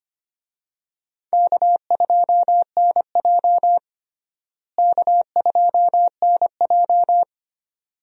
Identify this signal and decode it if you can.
Morse code, heard as a keyed tone, about 25 wpm, 700 Hz. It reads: K2NJ K2NJ
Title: USA Call Signs - 4 Characters - Repeated 2x - 25wpm